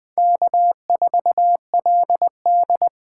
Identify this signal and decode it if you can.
Morse code, heard as a keyed tone, about 20 wpm, 700 Hz. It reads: K4LD
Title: USA Call Signs - 4x Character Spacing - 20wpm